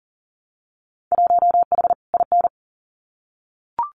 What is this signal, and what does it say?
Morse code, heard as a keyed tone, about 40 wpm, 700 Hz. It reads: 1H IN
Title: Winter Field Day Exchanges 40wpm